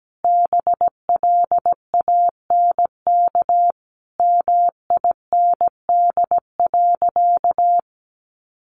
Morse code, heard as a keyed tone, about 17 wpm, 700 Hz. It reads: BLANK MIND.
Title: Sentences from Top 1500 Words 17wpm